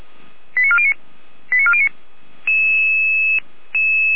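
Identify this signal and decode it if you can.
Five-tone selective call (ZVEI), heard as quick runs of five tones, tones of about 80 ms. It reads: 89409 89409
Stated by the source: Five-tone ZVEI selective call 89356 received on an SDR.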